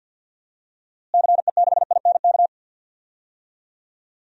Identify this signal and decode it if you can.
Morse code, heard as a keyed tone, about 47 wpm, 700 Hz. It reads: CE6INK
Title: DXCC-100 Call Signs - 6 Characters - Repeated 2x - 50wpm